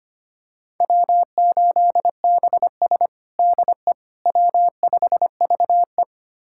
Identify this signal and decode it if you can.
Morse code, heard as a keyed tone, about 25 wpm, 700 Hz. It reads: W8BS DE W5VE
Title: USA Callsign DE Callsign - 4 Characters - 25wpm